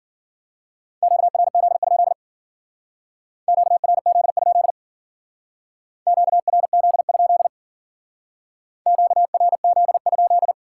Morse code, heard as a keyed tone, about 51 wpm, 700 Hz. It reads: QRZ? QRZ? QRZ? QRZ?